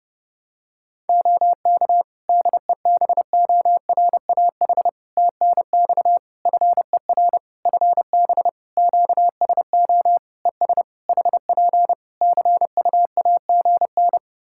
Morse code, heard as a keyed tone, about 30 wpm, 700 Hz. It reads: OK DEBORAH TNX FER FB QSO ES HP CUAGN
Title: Practice QSO - Exchange 4 - Ending - 30wpm